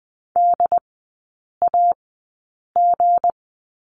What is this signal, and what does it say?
Morse code, heard as a keyed tone, about 20 wpm, spaced as for 8 wpm, 700 Hz. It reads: DAG
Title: 159 Lesson 26e - Letter g - Three Characters 20wpm